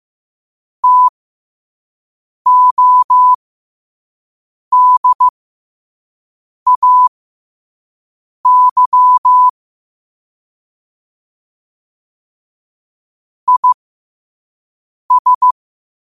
Morse code, heard as a keyed tone, about 15 wpm, spaced as for 5 wpm, 1000 Hz. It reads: TODAY IS
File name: qotdfiles / 5wpm